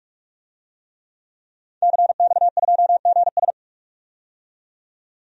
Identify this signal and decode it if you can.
Morse code, heard as a keyed tone, about 46 wpm, 700 Hz. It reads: CX2GS
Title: DXCC-100 Call Signs - 5 Characters - 45wpm